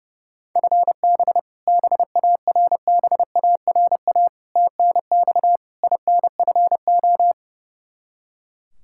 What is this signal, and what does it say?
Morse code, heard as a keyed tone, about 32 wpm, 700 Hz. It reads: FB BARBARA TNX INFO
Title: Practice QSO - Exchange 3 - More Info - 30wpm